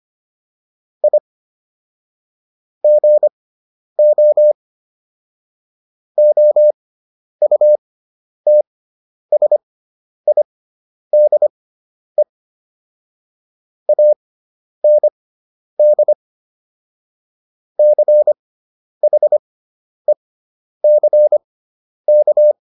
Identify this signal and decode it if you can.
Morse code, heard as a keyed tone, about 25 wpm, spaced as for 10 wpm, 600 Hz. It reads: I GO OUTSIDE AND CHECK